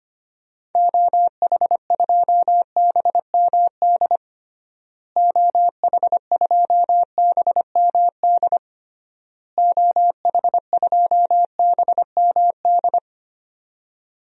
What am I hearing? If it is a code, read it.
Morse code, heard as a keyed tone, about 25 wpm, 700 Hz. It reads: OH2BMD OH2BMD OH2BMD
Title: DXCC-100 Call Signs - 6 Characters - Repeated 3x - 25wpm